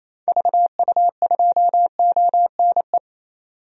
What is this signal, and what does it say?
Morse code, heard as a keyed tone, about 28 wpm, 700 Hz. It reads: VU2ONE
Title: DXCC-100 Call Signs - 6 Characters - 28wpm